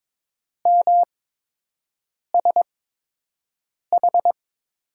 Morse code, heard as a keyed tone, about 22 wpm, spaced as for 6 wpm, 700 Hz. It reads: MSH
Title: USA Call Signs - 5 Characters - 8x Character Spacing - 22wpm